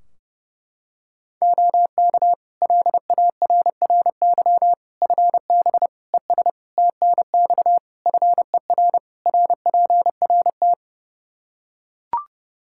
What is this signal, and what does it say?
Morse code, heard as a keyed tone, about 30 wpm, 700 Hz. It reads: OK LARRY FB ES TNX FER RPRT
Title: Practice QSO - Exchange 2 - Info - 30wpm